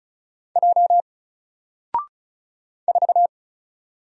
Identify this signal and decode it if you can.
Morse code, heard as a keyed tone, about 36 wpm, 700 Hz. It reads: J 4
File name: Single-Letter-Number-35wpm